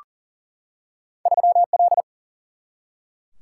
Morse code, heard as a keyed tone, about 40 wpm, 700 Hz. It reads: 3 L
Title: Single-Letter Single-Number 40wpm